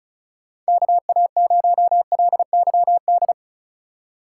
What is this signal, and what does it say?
Morse code, heard as a keyed tone, about 36 wpm, 700 Hz. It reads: KA0LYD
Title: USA Call Signs - 4x Character Spacing - 35wpm